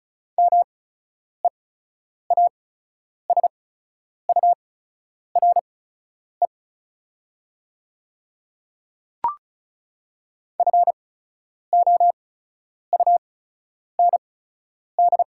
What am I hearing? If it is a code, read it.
Morse code, heard as a keyed tone, about 34 wpm, spaced as for 10 wpm, 690 Hz. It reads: MEASURE FOUND
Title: 8x Character Spacing - Top 500 Words - 35wpm